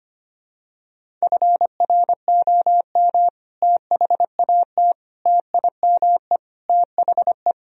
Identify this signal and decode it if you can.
Morse code, heard as a keyed tone, about 25 wpm, 700 Hz. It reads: FROM THAT TIME THE
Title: Sentences from Top 1000 Words - Review Part 1 - 25wpm